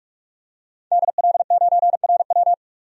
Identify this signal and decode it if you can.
Morse code, heard as a keyed tone, about 46 wpm, 700 Hz. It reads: DL9RW
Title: DXCC-100 Call Signs - 5 Characters - 45wpm